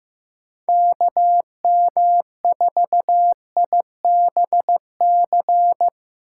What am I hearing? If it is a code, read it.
Morse code, heard as a keyed tone, about 15 wpm, 700 Hz. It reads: KM4IBC
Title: USA Call Signs - 4x Character Spacing - 15wpm